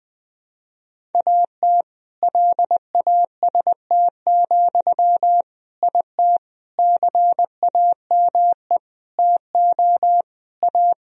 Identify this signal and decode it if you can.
Morse code, heard as a keyed tone, about 21 wpm, 700 Hz. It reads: AT LAST, IT CAME TO A
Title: Sentences from Top 1000 Words - Review Part 1 - 20wpm